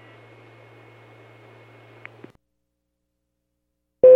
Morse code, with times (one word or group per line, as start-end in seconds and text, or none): none